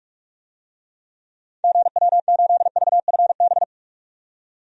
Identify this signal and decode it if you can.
Morse code, heard as a keyed tone, about 42 wpm, 700 Hz. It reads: GW8VFB
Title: DXCC-100 Call Signs - 6 Characters - 45wpm